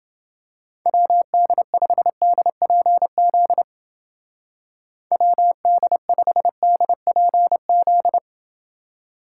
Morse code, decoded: WD5DPZ WD5DPZ

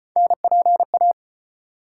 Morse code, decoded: NPA

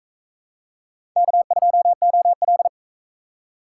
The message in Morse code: K2OL

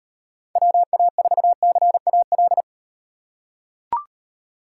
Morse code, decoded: WA4CAL

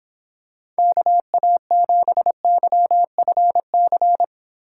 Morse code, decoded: KA7YFC

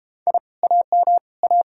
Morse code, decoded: I AM A

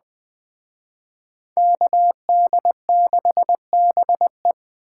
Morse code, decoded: KD6BE

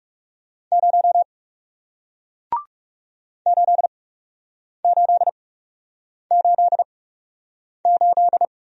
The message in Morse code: 0 8888

